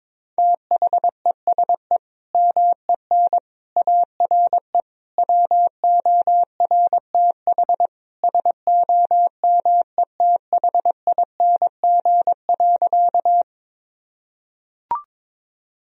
Morse code, heard as a keyed tone, about 22 wpm, 700 Hz. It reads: THESE MEN ARE WORTH SOMETHING.
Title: Sentences from Top 1000 Words - Review Part 1 - 22wpm